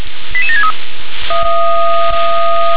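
Five-tone selective call (ZVEI), heard as quick runs of five tones, tones of about 70 ms. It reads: 88763
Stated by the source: ZVEI five-tone selective call received by SDR.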